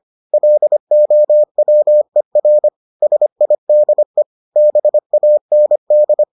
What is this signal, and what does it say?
Morse code, heard as a keyed tone, about 25 wpm, 600 Hz. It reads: LOWER SIDE BAND